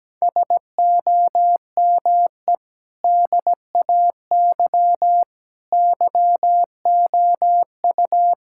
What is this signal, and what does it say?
Morse code, heard as a keyed tone, about 17 wpm, 700 Hz. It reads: SOME DAY YOU
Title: Sentences from Top 1000 Words - Review Part 6 - 17wpm